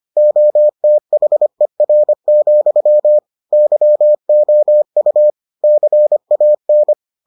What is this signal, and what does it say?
Morse code, heard as a keyed tone, about 25 wpm, 600 Hz. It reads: OTHER, YOU CAN